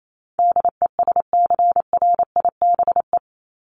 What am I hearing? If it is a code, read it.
Morse code, heard as a keyed tone, about 28 wpm, 700 Hz. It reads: DESCRIBE
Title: ED Suffix - Top 100 Words with Base Words 28wpm